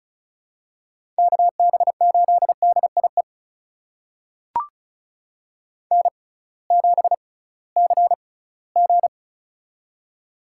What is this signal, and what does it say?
Morse code, heard as a keyed tone, about 36 wpm, spaced as for 20 wpm, 700 Hz. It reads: KB8DIE N7CG
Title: USA Call Signs - 6x Character Spacing - 35wpm